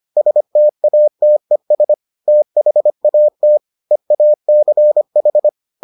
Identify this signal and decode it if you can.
Morse code, heard as a keyed tone, about 25 wpm, 600 Hz. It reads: STATES THAT EACH